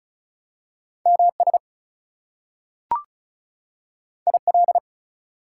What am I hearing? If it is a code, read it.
Morse code, heard as a keyed tone, about 36 wpm, 700 Hz. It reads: MS IL
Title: US States and Canadian Provinces 35wpm